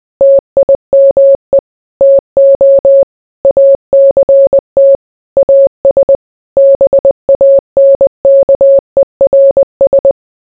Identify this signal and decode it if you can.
Morse code, heard as a keyed tone, about 20 wpm, 550 Hz. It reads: TIME TO ACT AS BANKERS